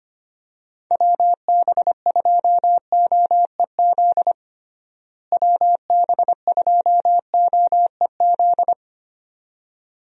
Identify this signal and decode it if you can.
Morse code, heard as a keyed tone, about 25 wpm, 700 Hz. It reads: WB2OEZ WB2OEZ
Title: USA Call Signs - 6 Characters - Repeated 2x - 25wpm